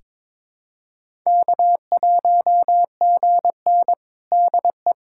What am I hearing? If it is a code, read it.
Morse code, heard as a keyed tone, about 21 wpm, 700 Hz. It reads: K1GN DE
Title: USA Callsign DE Callsign - 4 Characters - 22wpm